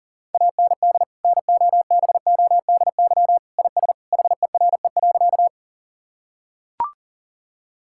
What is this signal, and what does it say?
Morse code, heard as a keyed tone, about 40 wpm, 700 Hz. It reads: AND NOBODY IS HERE.
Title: Sentences from Top 1600 Words 40wpm